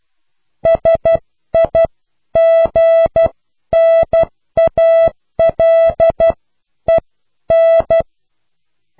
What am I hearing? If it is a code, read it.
Morse code, heard as a keyed tone, about 12 wpm, spaced as for 10 wpm, 660 Hz. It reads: SIGNALEN